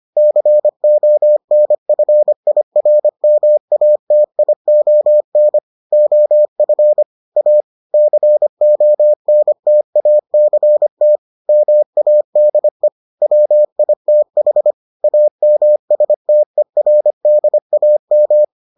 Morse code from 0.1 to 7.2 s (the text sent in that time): CONFIRMATION OF